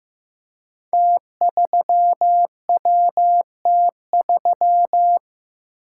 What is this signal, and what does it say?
Morse code, heard as a keyed tone, about 15 wpm, 700 Hz. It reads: T3WT3
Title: Code Groups - 5 Letter:Numbers - 15wpm